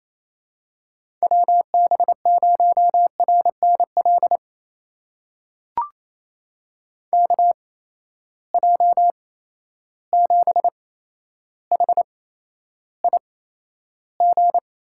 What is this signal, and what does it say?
Morse code, heard as a keyed tone, about 28 wpm, spaced as for 12 wpm, 700 Hz. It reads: WB0RNL KJ7HIG
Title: USA Call Signs - 6 Characters - 8x Character Spacing - 28wpm